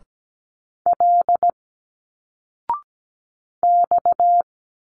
Morse code, heard as a keyed tone, about 17 wpm, 700 Hz. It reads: L X
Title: Single Letters 17wpm